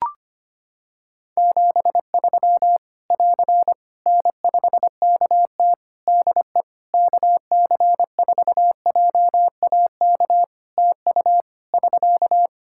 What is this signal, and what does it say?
Morse code, heard as a keyed tone, about 25 wpm, 700 Hz. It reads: 73 <AR> N5KT DE KC4JAK TU <SK>
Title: K1USN SST QSO Practice 25wpm